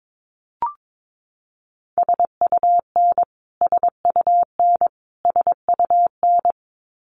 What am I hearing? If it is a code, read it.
Morse code, heard as a keyed tone, about 22 wpm, 700 Hz. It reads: SUN SUN SUN
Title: Top 500 Words Repeated 22wpm